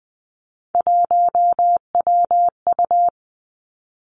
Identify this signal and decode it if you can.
Morse code, heard as a keyed tone, about 20 wpm, 700 Hz. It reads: 1WU
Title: Three Challenging Letter-Number Combinations 20wpm